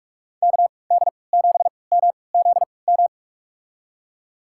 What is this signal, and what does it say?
Morse code, heard as a keyed tone, about 46 wpm, spaced as for 25 wpm, 700 Hz. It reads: KD7MZM
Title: USA Call Signs - 3x Character Spacing - 45wpm